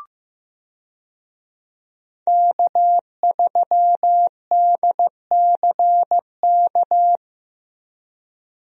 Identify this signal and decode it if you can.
Morse code, heard as a keyed tone, about 15 wpm, 700 Hz. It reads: K3DCK
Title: USA Call Signs - 5 Characters - Repeated 2x - 15wpm